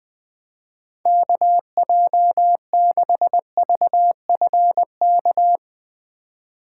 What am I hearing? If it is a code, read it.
Morse code, heard as a keyed tone, about 20 wpm, 700 Hz. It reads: KJ6VFK
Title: USA Call Signs - 6 Characters - 4x Character Spacing - 20wpm